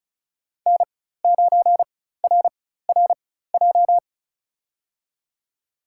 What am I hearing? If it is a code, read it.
Morse code, heard as a keyed tone, about 36 wpm, spaced as for 16 wpm, 700 Hz. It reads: N9RRJ
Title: USA Call Signs - 5 Characters - 4x Character Spacing - 35wpm